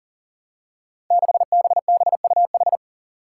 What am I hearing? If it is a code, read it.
Morse code, heard as a keyed tone, about 40 wpm, 700 Hz. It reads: 6BBUH
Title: Five Challenging Letter-Number Combinations 40wpm